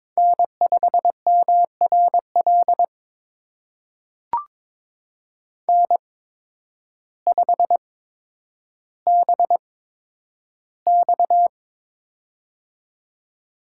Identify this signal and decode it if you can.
Morse code, heard as a keyed tone, about 23 wpm, spaced as for 10 wpm, 700 Hz. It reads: N5MRL N5BX
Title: USA Call Signs - 8x Character Spacing - 22wpm